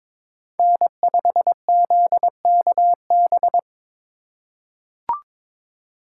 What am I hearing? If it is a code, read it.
Morse code, heard as a keyed tone, about 22 wpm, 700 Hz. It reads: N5ZKB
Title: USA Call Signs - 5 Characters - 3x Character Spacing - 22wpm